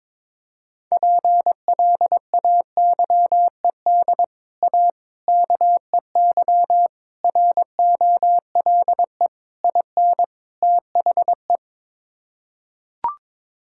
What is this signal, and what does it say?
Morse code, heard as a keyed tone, about 22 wpm, 700 Hz. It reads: PLAYED A KEY ROLE IN THE
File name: practice / 22wpm / Sets of 6 Words - 22wpm